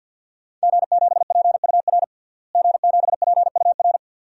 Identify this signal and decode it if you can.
Morse code, heard as a keyed tone, about 50 wpm, 700 Hz. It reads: G7PUR G7PUR
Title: DXCC-100 Call Signs - 5 Characters - Repeated 3x - 50wpm